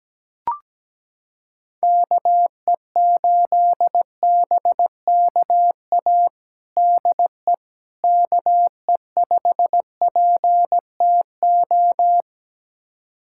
Morse code, heard as a keyed tone, about 17 wpm, 700 Hz. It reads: KE8BKA DE KE5PTO
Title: USA Callsign DE Callsign - 6 Characters - 17wpm